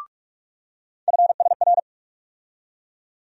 Morse code, heard as a keyed tone, about 43 wpm, 700 Hz. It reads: FSR
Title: USA Call Sign Postfixes - 3 Letters 45wpm